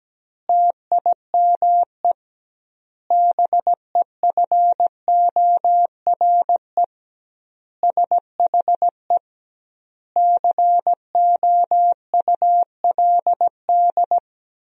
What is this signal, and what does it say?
Morse code, heard as a keyed tone, about 17 wpm, 700 Hz. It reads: TIME BEFORE SHE COULD
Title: 2x Word-Spacing - Sentences from Top 500 Words - 17wpm